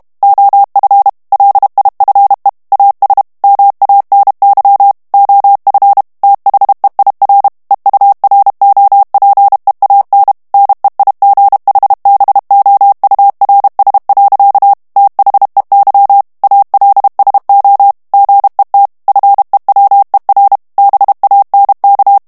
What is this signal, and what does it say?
Morse code, heard as a keyed tone, about 32 wpm, 800 Hz. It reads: OF LIFE AS MANY OF THEIR EUROPEAN NEIGHBOURS. THEY ALSO GET FEWER BANK
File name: morse mp3 / ANGLAIS texte clair 1 vitesse 30 mots minute